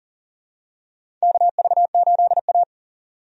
Morse code, heard as a keyed tone, about 40 wpm, 700 Hz. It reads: KV8A
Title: USA Call Signs - 4 Characters - 40wpm